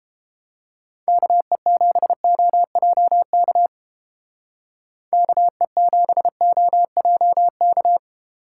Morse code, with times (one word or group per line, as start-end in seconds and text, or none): 1.1-3.7 KE7OJK
5.1-8.0 KE7OJK